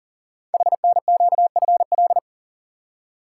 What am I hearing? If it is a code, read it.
Morse code, heard as a keyed tone, about 38 wpm, 700 Hz. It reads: HNQFL
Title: Code Groups - 5 Letters - 40wpm